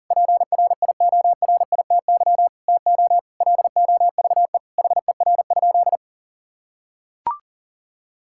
Morse code, heard as a keyed tone, about 40 wpm, 700 Hz. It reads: PRIORITY TO LOVE HER?
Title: Sentences from Top 1600 Words 40wpm